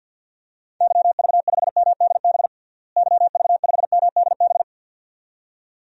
Morse code, heard as a keyed tone, about 53 wpm, 700 Hz. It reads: YV5MDB YV5MDB
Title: DXCC-100 Call Signs - 6 Characters - Repeated 2x - 50wpm